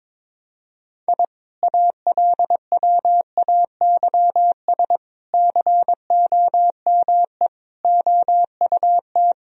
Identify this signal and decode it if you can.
Morse code, heard as a keyed tone, about 21 wpm, 700 Hz. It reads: I ALWAYS COME OUT
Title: Sentences from Top 1000 Words - Review Part 2 - 22wpm